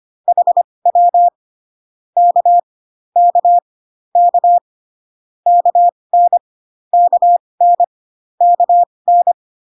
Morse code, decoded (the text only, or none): HW K K K KN KN KN